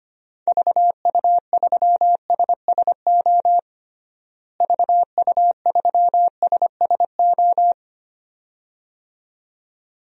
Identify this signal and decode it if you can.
Morse code, heard as a keyed tone, about 25 wpm, 700 Hz. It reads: VU3SSO VU3SSO
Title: DXCC-100 Call Signs - 6 Characters - Repeated 2x - 25wpm